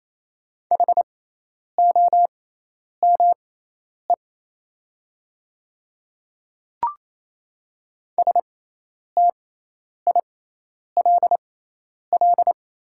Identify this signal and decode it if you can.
Morse code, heard as a keyed tone, about 27 wpm, spaced as for 10 wpm, 700 Hz. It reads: HOME STILL
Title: 6x Character Spacing - Top 300 Words - 28wpm